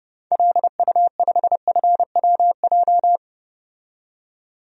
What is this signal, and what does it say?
Morse code, heard as a keyed tone, about 30 wpm, 700 Hz. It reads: LU5FWJ
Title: DXCC-100 Call Signs - 6 Characters - 30wpm